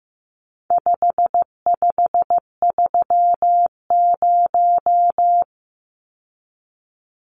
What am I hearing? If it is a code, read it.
Morse code, heard as a keyed tone, about 15 wpm, 700 Hz. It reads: 5530
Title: Numbers - 4-Digit 15wpm